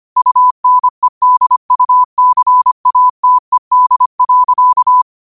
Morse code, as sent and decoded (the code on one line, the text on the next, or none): ..- -. -.. ..- -.-. .- - . -.. .-.-.-
UNEDUCATED.